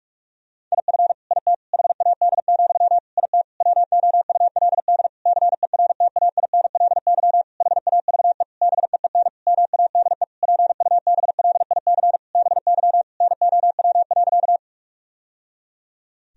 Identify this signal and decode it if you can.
Morse code, heard as a keyed tone, about 46 wpm, 700 Hz. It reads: IF IT HAD, IT WOULD CERTAINLY HAVE BEEN MADE PUBLIC BY NOW.
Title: Sentences from Top 1000 Words - Review Part 3 - 45wpm